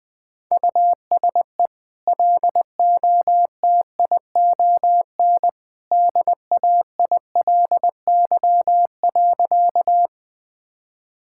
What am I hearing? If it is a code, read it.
Morse code, heard as a keyed tone, about 20 wpm, 700 Hz. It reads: USE LOTION DAILY.
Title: Easy Sentences with Hints - 20wpm